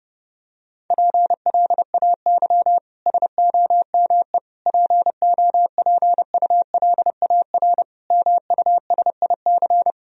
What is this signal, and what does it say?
Morse code, heard as a keyed tone, about 32 wpm, 700 Hz. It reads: PLAY SOME POPULAR MUSIC
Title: Sentences from Top 1000 Words - Review Part 3 - 30wpm